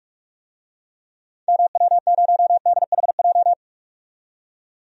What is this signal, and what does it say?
Morse code, heard as a keyed tone, about 47 wpm, 700 Hz. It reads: MW0DHJ
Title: DXCC-100 Call Signs - 6 Characters - 45wpm